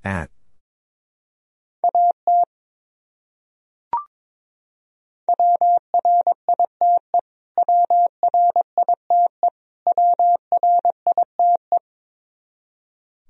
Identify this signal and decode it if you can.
Morse code, heard as a keyed tone, about 21 wpm, 700 Hz. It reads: AT WRITE WRITE WRITE
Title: Top 100 Words Repeated 22wpm